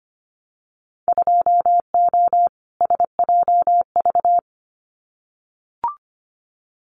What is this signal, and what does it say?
Morse code, heard as a keyed tone, about 25 wpm, 700 Hz. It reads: 2O SJV
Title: Winter Field Day Exchanges Encore 25wpm